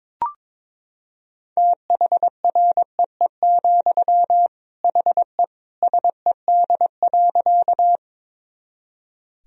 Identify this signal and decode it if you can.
Morse code, heard as a keyed tone, about 22 wpm, 700 Hz. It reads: THREE, HE SED.
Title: Sentences from Top 200 Words and Abbreviations 22wpm